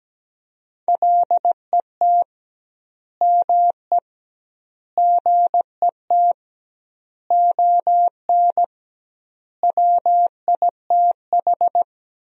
Morse code, heard as a keyed tone, about 17 wpm, 700 Hz. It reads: LET ME GET ON WITH
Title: 2x Word-Spacing - Sentences from Top 300 Words - 17wpm